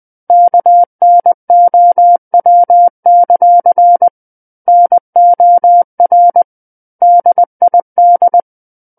Morse code, heard as a keyed tone, about 20 wpm, 700 Hz. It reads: KNOW; NOR DID
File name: 20 WPM